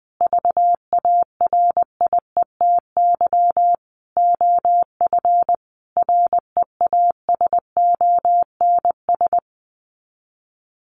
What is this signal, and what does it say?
Morse code, heard as a keyed tone, about 20 wpm, 700 Hz. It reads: VARIETY OF REASONS
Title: Sets of 5 Words Encore 20wpm